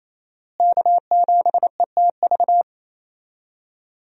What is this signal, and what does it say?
Morse code, heard as a keyed tone, about 28 wpm, 700 Hz. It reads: K7ETV
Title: USA Call Signs - 5 Characters - Repeated 3x - 28wpm